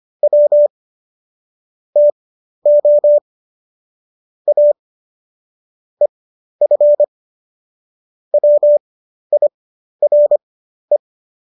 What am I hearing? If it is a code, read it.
Morse code, heard as a keyed tone, about 25 wpm, spaced as for 12 wpm, 590 Hz. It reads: W TO A EF WIRE